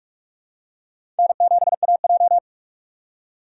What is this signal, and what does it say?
Morse code, heard as a keyed tone, about 46 wpm, 700 Hz. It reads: N7AJ
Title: USA Call Signs - 4 Characters - 2x Character Spacing - 45wpm